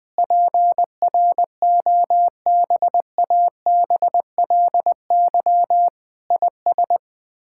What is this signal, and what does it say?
Morse code, heard as a keyed tone, about 20 wpm, 700 Hz. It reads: PROBABLY IS
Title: Sets of 9 Words - 20wpm